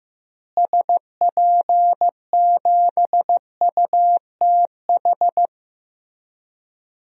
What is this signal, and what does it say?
Morse code, heard as a keyed tone, about 15 wpm, 700 Hz. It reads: SP7UTH
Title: DXCC-100 Call Signs - 6 Characters - Repeated 2x - 15wpm